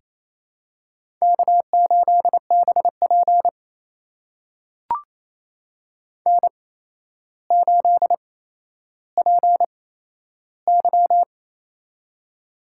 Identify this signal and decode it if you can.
Morse code, heard as a keyed tone, about 28 wpm, spaced as for 12 wpm, 700 Hz. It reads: K8BP N8PY
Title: USA Call Signs - 4 Characters - 8x Character Spacing - 28wpm